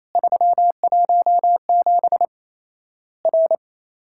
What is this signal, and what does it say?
Morse code, heard as a keyed tone, about 28 wpm, 700 Hz. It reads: 317 R
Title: Worked All Europe DX Contest - QTC Exchange - 28wpm